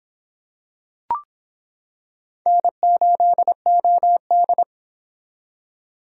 Morse code, heard as a keyed tone, about 26 wpm, 700 Hz. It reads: N8OD